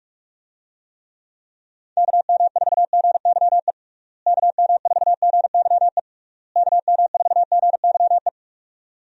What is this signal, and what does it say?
Morse code, heard as a keyed tone, about 41 wpm, 700 Hz. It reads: KM4GYE KM4GYE KM4GYE
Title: USA Call Signs - 6 Characters - Repeated 3x - 45wpm